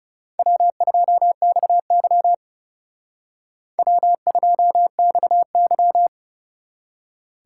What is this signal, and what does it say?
Morse code, heard as a keyed tone, about 36 wpm, 700 Hz. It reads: W2XY W2XY
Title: Speed Racing - USA 1x2 Call Signs 30wpm